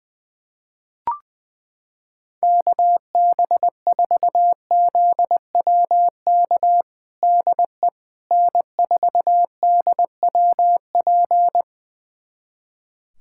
Morse code, heard as a keyed tone, about 20 wpm, 700 Hz. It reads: KB4ZWK DE N4DWP
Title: USA Callsign DE Callsign 20wpm